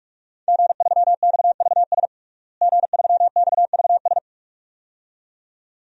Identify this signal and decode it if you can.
Morse code, heard as a keyed tone, about 44 wpm, 700 Hz. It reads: G3XVS G3XVS
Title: DXCC-100 Call Signs - 5 Characters - Repeated 2x - 45wpm